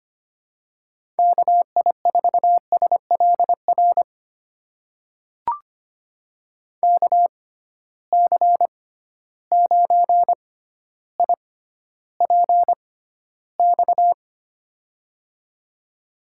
Morse code, heard as a keyed tone, about 25 wpm, spaced as for 13 wpm, 700 Hz. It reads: KI4SLR KC9IPX
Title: USA Call Signs - 6 Characters - 6x Character Spacing - 25wpm